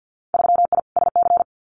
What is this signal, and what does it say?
Morse code, heard as a keyed tone, about 50 wpm, 700 Hz. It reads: 3I SC
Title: Winter Field Day Exchanges 50wpm